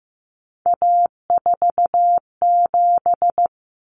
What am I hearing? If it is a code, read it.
Morse code, heard as a keyed tone, about 15 wpm, 700 Hz. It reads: A47
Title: Common Cut Numbers - 3-Digit 15wpm